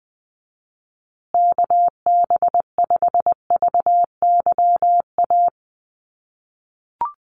Morse code, heard as a keyed tone, about 20 wpm, 700 Hz. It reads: KB5VYA
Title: USA Call Signs Repeated 2x 20wpm